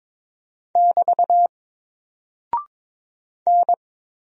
Morse code, heard as a keyed tone, about 22 wpm, 700 Hz. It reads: <BT> N
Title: Single Character (Letter:Number:Punctuation:Prosign) 22wpm